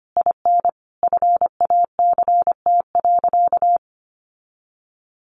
Morse code, heard as a keyed tone, about 25 wpm, 700 Hz. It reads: IN FACT.
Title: Sentences from Top 1200 Words 25wpm